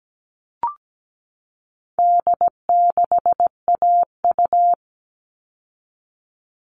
Morse code, heard as a keyed tone, about 17 wpm, 700 Hz. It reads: D6AU
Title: All Cut Numbers - 4-Digit 17wpm